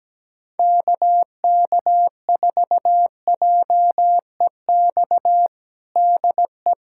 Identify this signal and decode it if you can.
Morse code, heard as a keyed tone, about 17 wpm, 700 Hz. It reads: KK4JEX DE
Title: USA Callsign DE Callsign - 6 Characters - 17wpm